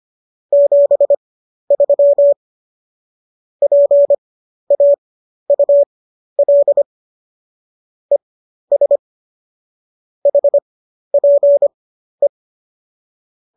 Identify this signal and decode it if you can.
Morse code, heard as a keyed tone, about 25 wpm, spaced as for 12 wpm, 570 Hz. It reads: 73 PAUL ES HPE